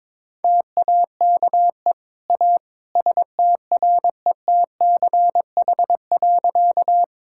Morse code, heard as a keyed tone, about 22 wpm, 700 Hz. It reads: TAKE A STRETCH.
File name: practice / 22wpm / Easy Sentences with Hints - 22wpm